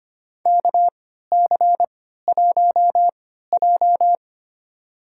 Morse code, heard as a keyed tone, about 25 wpm, spaced as for 14 wpm, 700 Hz. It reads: KC1J
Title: USA Call Signs - 4 Characters - 3x Character Spacing - 25wpm